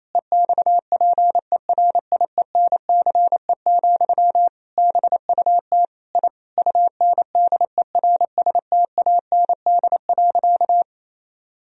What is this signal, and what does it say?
Morse code, heard as a keyed tone, about 28 wpm, 700 Hz. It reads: EXPERIENCE, BUT I UNDERSTAND.
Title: Sentences from Top 1000 Words - Review Part 4 - 28wpm